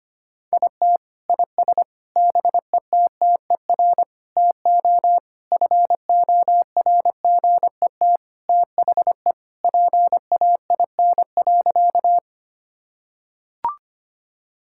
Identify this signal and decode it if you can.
Morse code, heard as a keyed tone, about 25 wpm, 700 Hz. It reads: IT IS BETTER TO FORGET THE PAIN.
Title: Sentences from Top 1600 Words 25wpm